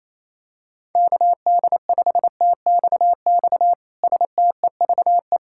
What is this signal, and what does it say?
Morse code, heard as a keyed tone, about 27 wpm, 700 Hz. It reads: KD5TXX STEVE
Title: K1USN SST QSO Practice - 28wpm